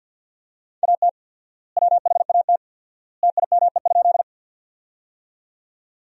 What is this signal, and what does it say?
Morse code, heard as a keyed tone, about 50 wpm, 700 Hz. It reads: AT WHAT TIME?
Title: 4x Word-Spacing - Sentences from Top 100 Words - 50wpm